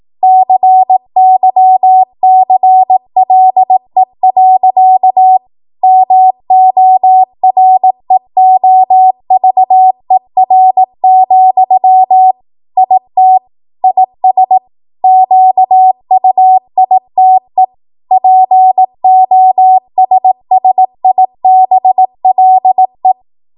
Morse code, heard as a keyed tone, about 18 wpm, 750 Hz. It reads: CYCLE. MOREOVER, IT IS QUITE POSSIBLE
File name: Morse / 18WPM